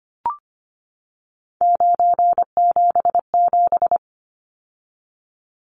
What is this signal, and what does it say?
Morse code, heard as a keyed tone, about 25 wpm, 700 Hz. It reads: 977
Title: Numbers - 3-Digit 25wpm